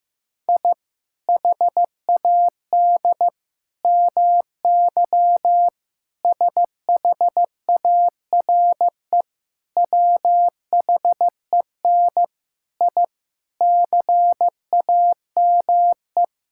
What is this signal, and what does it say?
Morse code, heard as a keyed tone, about 15 wpm, 700 Hz. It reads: I HAD MY SHARE WHEN I CAME
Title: Sentences from Top 1000 Words - Review Part 6 - 15wpm